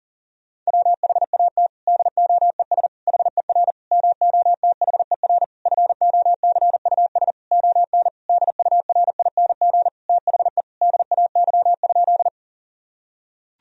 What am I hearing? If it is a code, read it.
Morse code, heard as a keyed tone, about 40 wpm, 700 Hz. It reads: WHAT DOES HER MOTHER FOCUS ON DURING THE DAY?
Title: Sentences from Top 1000 Words - Review Part 4 - 40wpm